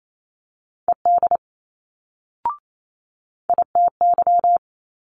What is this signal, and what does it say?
Morse code, heard as a keyed tone, about 28 wpm, 700 Hz. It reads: ED ITY
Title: Multi-letter Phonemes Quick Fire 28wpm